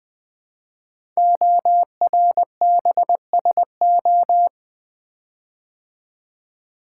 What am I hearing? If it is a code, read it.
Morse code, heard as a keyed tone, about 20 wpm, 700 Hz. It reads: ORBSO
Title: Code Groups - 5 Letter:Numbers - 20wpm